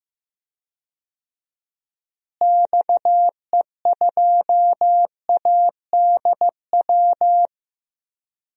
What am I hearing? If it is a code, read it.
Morse code, heard as a keyed tone, about 16 wpm, 700 Hz. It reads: XE2ADW
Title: DXCC-100 Call Signs - 6 Characters - Repeated 3x - 15wpm